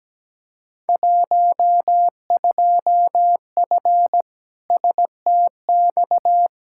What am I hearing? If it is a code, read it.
Morse code, heard as a keyed tone, about 17 wpm, 700 Hz. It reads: 12F STX
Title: Field Day Exchanges Encore - 17wpm